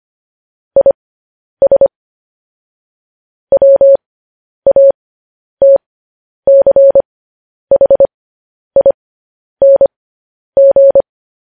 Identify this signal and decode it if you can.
Morse code, heard as a keyed tone, about 25 wpm, spaced as for 10 wpm, 550 Hz. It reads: IS WATCHING
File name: podcast / News headlines-1x-25x10